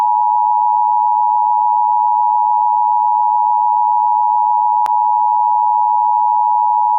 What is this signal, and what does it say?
Emergency Alert System (SAME) tones (the 853 and 960 Hz attention tone). It is heard from 0.4 to 6.4 s.